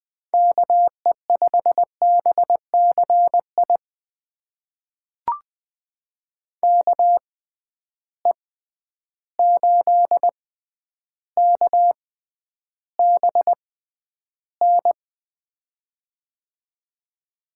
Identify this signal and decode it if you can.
Morse code, heard as a keyed tone, about 20 wpm, spaced as for 10 wpm, 700 Hz. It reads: KE5BCI KE8KBN
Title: USA Call Signs - 6 Characters - 6x Character Spacing - 20wpm